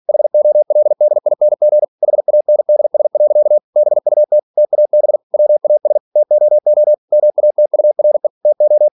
Morse code, heard as a keyed tone, about 46 wpm, 600 Hz. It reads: HOLDING HANDS. BUT TAB WAS TOO MATURE TO
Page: Never 47wpm